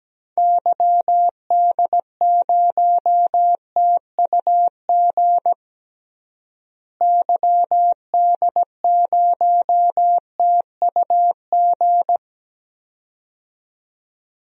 Morse code, heard as a keyed tone, about 17 wpm, 700 Hz. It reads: YD0TUG YD0TUG
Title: DXCC-100 Call Signs - 6 Characters - Repeated 2x - 17wpm